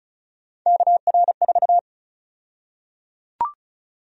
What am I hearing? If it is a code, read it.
Morse code, heard as a keyed tone, about 34 wpm, 700 Hz. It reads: KR4
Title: USA Call Sign Prefixes - Repeated 2x - 35wpm